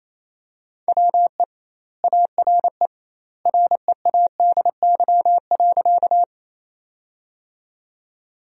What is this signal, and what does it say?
Morse code, heard as a keyed tone, about 28 wpm, 700 Hz. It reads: WE ARE READY.
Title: 2x Word-Spacing - Sentences from Top 400 Words - 28wpm